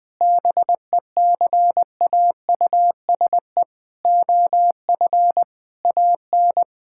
Morse code, heard as a keyed tone, about 20 wpm, 700 Hz. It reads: BECAUSE OF AN
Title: Sentences from Top 1000 Words - Review Part 6 - 20wpm